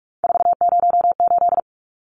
Morse code, heard as a keyed tone, about 44 wpm, 700 Hz. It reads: 408
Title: Numbers - 3-Digit 45wpm